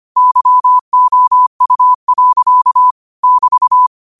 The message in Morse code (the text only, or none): YOU. <BT>